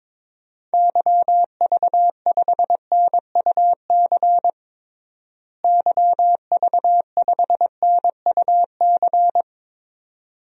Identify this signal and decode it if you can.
Morse code, heard as a keyed tone, about 22 wpm, 700 Hz. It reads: YV5NUC YV5NUC
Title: DXCC-100 Call Signs - 6 Characters - Repeated 2x - 22wpm